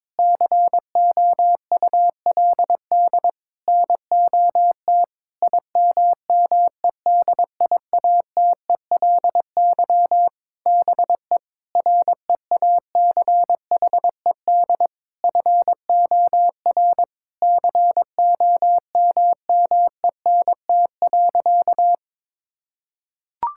Morse code, heard as a keyed tone, about 22 wpm, 700 Hz. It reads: COULD NOT IMMEDIATELY BE REACHED FOR COMMENT.
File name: Sentences with 7-Word Phrases - 22wpm